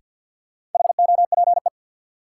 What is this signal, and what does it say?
Morse code, heard as a keyed tone, about 52 wpm, 700 Hz. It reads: HOPE
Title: Sentences from Top 1000 Words - New Word Drill - 50wpm